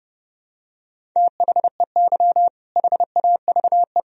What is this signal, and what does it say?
Morse code, heard as a keyed tone, about 32 wpm, 700 Hz. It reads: THEY HAVE
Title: Sentences from Top 1000 Words - Review Part 1 - 30wpm